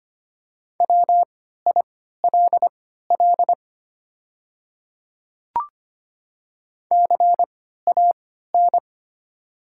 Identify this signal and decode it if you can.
Morse code, heard as a keyed tone, about 25 wpm, spaced as for 14 wpm, 700 Hz. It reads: WILL CAN
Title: 3x Character Spacing - Top 300 Words - 25wpm